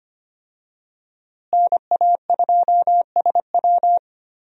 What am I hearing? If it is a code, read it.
Morse code, heard as a keyed tone, about 25 wpm, 700 Hz. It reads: NA2SW
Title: USA Call Signs - 5 Characters - Repeated 3x - 25wpm